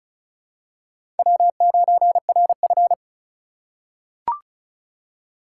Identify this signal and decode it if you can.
Morse code, heard as a keyed tone, about 34 wpm, 700 Hz. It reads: W9RF
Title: USA Call Signs - 4 Characters - 3x Character Spacing - 35wpm